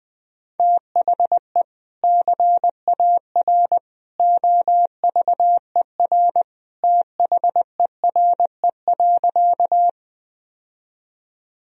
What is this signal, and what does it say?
Morse code, heard as a keyed tone, about 19 wpm, 700 Hz. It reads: THE CAR OVER THERE.
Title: Sentences from Top 1000 Words - Review Part 1 - 20wpm